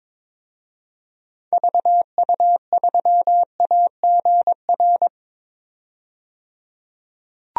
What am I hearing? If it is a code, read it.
Morse code, heard as a keyed tone, about 22 wpm, 700 Hz. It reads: VU3AGR
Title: DXCC-100 Call Signs - 6 Characters - 22wpm